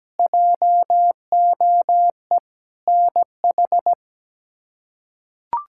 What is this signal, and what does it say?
Morse code, heard as a keyed tone, about 17 wpm, 700 Hz. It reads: JOE NH
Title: K1USN SST QSO Practice - 17wpm